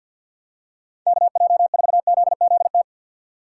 Morse code, heard as a keyed tone, about 52 wpm, 700 Hz. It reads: KJ4ZZT
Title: USA Call Signs - 6 Characters - 3x Character Spacing - 50wpm